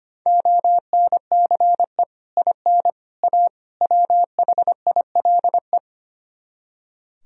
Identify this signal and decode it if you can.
Morse code, heard as a keyed tone, about 33 wpm, 700 Hz. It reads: ONCE IN A WHILE